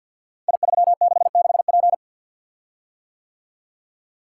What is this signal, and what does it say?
Morse code, heard as a keyed tone, about 47 wpm, 700 Hz. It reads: I366P
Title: Code Groups - 5 Letter:Numbers - 50wpm